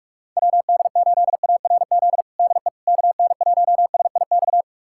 Morse code, heard as a keyed tone, about 46 wpm, 700 Hz. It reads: WD8ARZ DE KN1SIX
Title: USA Callsign DE Callsign - 6 Characters - 45wpm